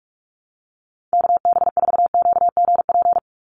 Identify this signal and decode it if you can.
Morse code, heard as a keyed tone, about 44 wpm, 700 Hz. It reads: KB4QGP
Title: USA Call Signs 45wpm